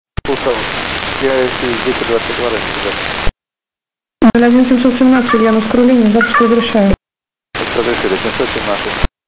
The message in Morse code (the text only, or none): none